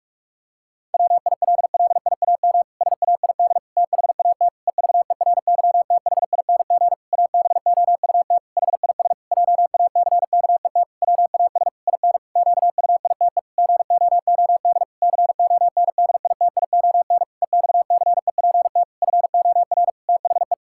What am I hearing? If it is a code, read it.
Morse code, heard as a keyed tone, about 44 wpm, 700 Hz. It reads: WILLIAM SAID THAT EVERYTHING ABOUT HIS JACKET WAS IN QUITE GOOD CONDITION EXCEPT FOR THE
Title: Pangram Challenge v2 - 45wpm